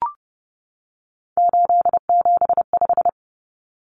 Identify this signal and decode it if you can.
Morse code, heard as a keyed tone, about 30 wpm, 700 Hz. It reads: 875
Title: Numbers - 3-Digit 30wpm